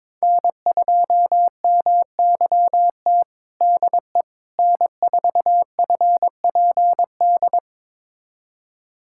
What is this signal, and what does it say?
Morse code, heard as a keyed tone, about 22 wpm, 700 Hz. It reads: N2MYT DE N4FPD
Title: USA Callsign DE Callsign - 5 Characters - 22wpm